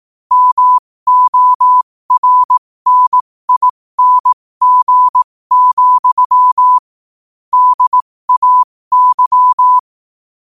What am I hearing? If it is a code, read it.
Morse code, heard as a keyed tone, about 18 wpm, spaced as for 15 wpm, 1000 Hz. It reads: MORNING, DAY
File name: qotdfiles / 15wpm